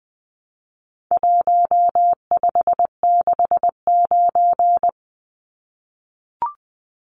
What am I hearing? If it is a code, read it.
Morse code, heard as a keyed tone, about 20 wpm, 700 Hz. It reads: 1569
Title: Numbers - Contest Style - 4-Digit 20wpm